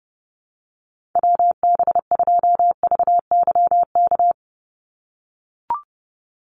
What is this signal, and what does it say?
Morse code, heard as a keyed tone, about 30 wpm, 700 Hz. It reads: WB2VYK
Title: Call Signs - 30wpm